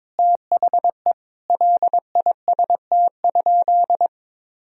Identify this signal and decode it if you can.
Morse code, heard as a keyed tone, about 22 wpm, 700 Hz. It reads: THE LIST?
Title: Sentences from Top 1000 Words - Review Part 2 - 22wpm